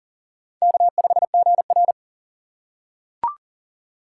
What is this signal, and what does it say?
Morse code, heard as a keyed tone, about 37 wpm, 700 Hz. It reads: K5GR
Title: USA Call Signs - 4 Characters - 2x Character Spacing - 40wpm